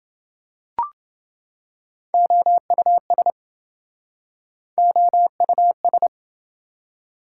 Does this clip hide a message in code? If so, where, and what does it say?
Yes: Morse code from 2.1 to 6.1 s, heard as a keyed tone, about 27 wpm, 700 Hz. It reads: OUS OUS